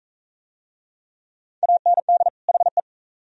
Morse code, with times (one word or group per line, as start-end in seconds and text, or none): 1.6-2.3 AND
2.5-2.8 HE